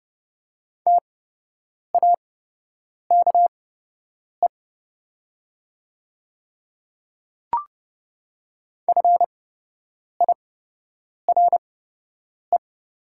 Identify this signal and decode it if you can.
Morse code, heard as a keyed tone, about 29 wpm, spaced as for 8 wpm, 700 Hz. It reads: TAKE FIRE
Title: 8x Character Spacing - Top 500 Words - 30wpm